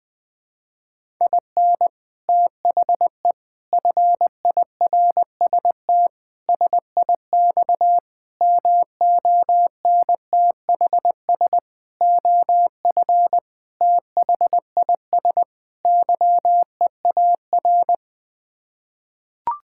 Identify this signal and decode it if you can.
Morse code, heard as a keyed tone, about 20 wpm, 700 Hz. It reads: IN THE FIRST SIX MONTHS OF THIS YEAR
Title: Sets of 8 Words - 20wpm